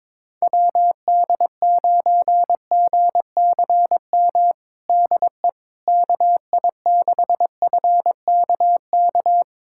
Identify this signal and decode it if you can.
Morse code, heard as a keyed tone, about 21 wpm, 700 Hz. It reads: WD9GCM DE KI6FKK
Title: USA Callsign DE Callsign - 6 Characters - 22wpm